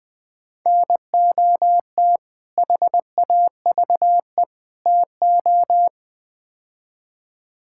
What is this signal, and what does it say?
Morse code, heard as a keyed tone, about 20 wpm, 700 Hz. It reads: NOT HAVE TO
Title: Build A Phrase - 4-Word Phrases - 20wpm